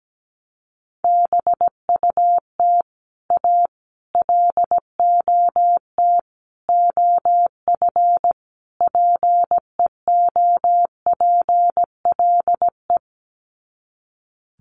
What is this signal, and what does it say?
Morse code, heard as a keyed tone, about 17 wpm, 700 Hz. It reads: BUT A LOT OF PEOPLE
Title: Sets of 5 Words Encore2 17wpm